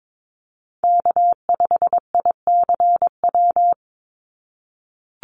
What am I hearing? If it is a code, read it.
Morse code, heard as a keyed tone, about 22 wpm, 700 Hz. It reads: K5ICW
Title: USA Call Signs 22wpm